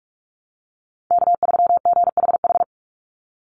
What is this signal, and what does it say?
Morse code, heard as a keyed tone, about 44 wpm, 700 Hz. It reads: K3GHH
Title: Call Signs - 45wpm